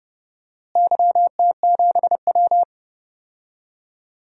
Morse code, heard as a keyed tone, about 31 wpm, 700 Hz. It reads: YT7W
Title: DXCC-100 Call Signs - 4 Characters - Repeated 2x - 30wpm